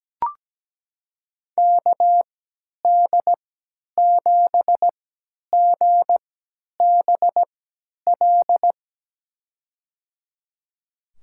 Morse code, heard as a keyed tone, about 17 wpm, spaced as for 10 wpm, 700 Hz. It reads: KD7GBL
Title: USA Call Signs - 6 Characters - 3x Character Spacing - 17wpm